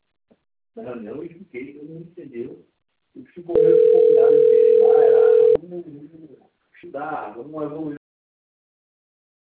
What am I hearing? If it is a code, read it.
Ringback tone (North America), 440 + 480 Hz, with speech.
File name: out- - - - - .81884